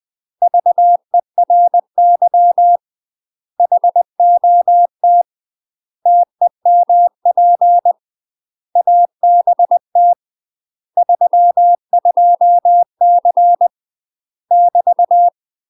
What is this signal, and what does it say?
Morse code, heard as a keyed tone, about 20 wpm, 700 Hz. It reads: VERY HOT TEMP ABT 32C <BT>
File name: UK MASTER QSOs TRAINING / 20WPM / UK MASTER QSOs 0071